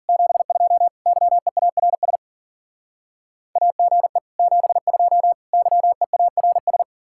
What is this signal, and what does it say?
Morse code, heard as a keyed tone, about 51 wpm, 700 Hz. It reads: 72 YEARS AGE 72 YEARS